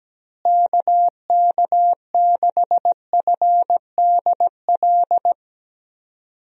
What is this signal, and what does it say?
Morse code, heard as a keyed tone, about 17 wpm, 700 Hz. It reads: KK6FDL
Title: USA Call Signs - 6 Characters - Repeated 3x - 17wpm